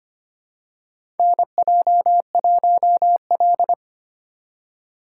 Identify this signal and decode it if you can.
Morse code, heard as a keyed tone, about 25 wpm, 700 Hz. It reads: NJ1L
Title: USA Call Signs - 8x Character Spacing - 25wpm